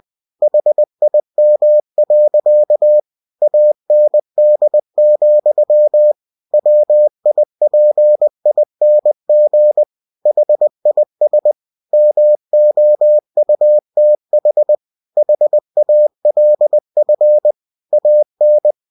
Morse code, heard as a keyed tone, about 20 wpm, 600 Hz. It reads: HIM. AND, WIPING HIS MOUTH HALF AN
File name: cw-books / winnie-the-pooh / winnie-20wpm-page-0032